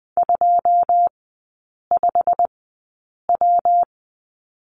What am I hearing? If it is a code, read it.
Morse code, heard as a keyed tone, about 20 wpm, spaced as for 8 wpm, 700 Hz. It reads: 25W
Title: 152 Lesson 25e - Letter p - Three Characters 20wpm